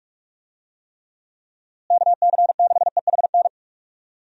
Morse code, heard as a keyed tone, about 46 wpm, 700 Hz. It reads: KC6EHN
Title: USA Call Signs - 6 Characters - Repeated 3x - 45wpm